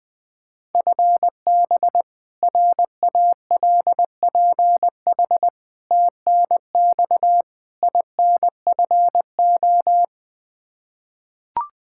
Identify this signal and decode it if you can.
Morse code, heard as a keyed tone, about 20 wpm, 700 Hz. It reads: FB RALPH TNX INFO
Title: Practice QSO - Exchange 3 - More Info - 20wpm